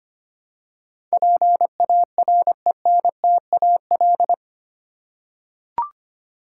Morse code, heard as a keyed tone, about 25 wpm, 700 Hz. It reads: PARENTAL
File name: Eight Letter Words - 25wpm